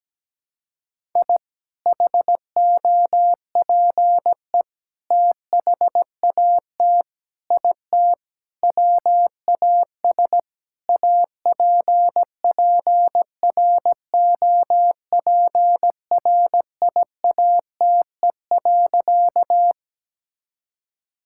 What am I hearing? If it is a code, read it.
Morse code, heard as a keyed tone, about 17 wpm, 700 Hz. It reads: I HOPE THAT IT WAS APPROPRIATE.
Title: Sentences from Top 1000 Words - Review Part 5 - 17wpm